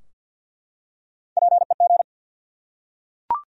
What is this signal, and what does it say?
Morse code, heard as a keyed tone, about 52 wpm, 700 Hz. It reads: PEG
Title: USA Common Names - Max 3 Letters - 50wpm